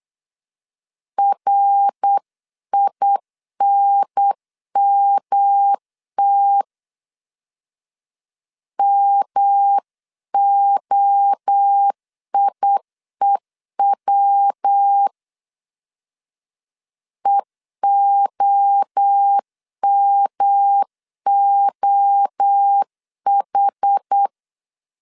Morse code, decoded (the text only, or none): RINMT MOIEW EOMOH